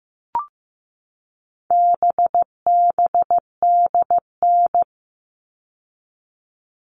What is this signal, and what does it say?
Morse code, heard as a keyed tone, about 15 wpm, 700 Hz. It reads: BBDN
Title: All Cut Numbers - 4-Digit 15wpm